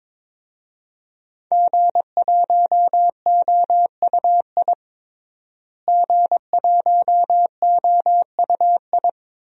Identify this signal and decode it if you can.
Morse code, heard as a keyed tone, about 22 wpm, 700 Hz. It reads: G1OUI G1OUI
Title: DXCC-100 Call Signs - 5 Characters - Repeated 2x - 22wpm